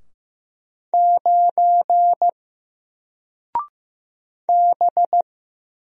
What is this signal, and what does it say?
Morse code, heard as a keyed tone, about 15 wpm, 700 Hz. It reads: 9 B
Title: Single Character (Letter:Number:Punctuation:Prosign) Alt 15wpm